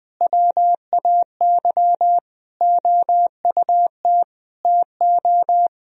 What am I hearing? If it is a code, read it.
Morse code, heard as a keyed tone, about 20 wpm, 700 Hz. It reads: WAY OUT TO
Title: Sentences from Top 1000 Words - Review Part 2 - 20wpm